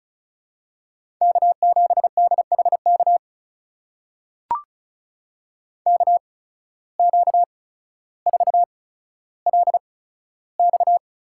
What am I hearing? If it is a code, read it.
Morse code, heard as a keyed tone, about 36 wpm, spaced as for 15 wpm, 700 Hz. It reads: K7DHK KQ4LX
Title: USA Call Signs - 5 Characters - 8x Character Spacing - 35wpm